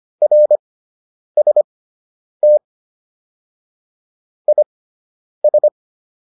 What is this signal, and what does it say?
Morse code, heard as a keyed tone, about 25 wpm, spaced as for 9 wpm, 600 Hz. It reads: RST IS